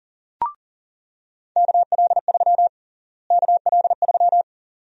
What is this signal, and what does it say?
Morse code, heard as a keyed tone, about 37 wpm, 700 Hz. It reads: KL3 KL3
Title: USA Call Sign Prefixes - Repeated 2x - 40wpm